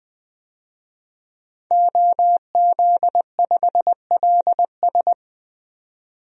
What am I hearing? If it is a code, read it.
Morse code, heard as a keyed tone, about 20 wpm, 700 Hz. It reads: OZ5LS
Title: DXCC-100 Call Signs - 5 Characters - 20wpm